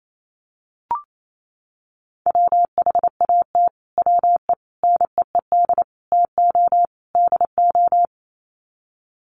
Morse code, heard as a keyed tone, about 28 wpm, 700 Hz. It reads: WHAT WE NEED TO DO
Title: Sets of 5 Words Encore 28wpm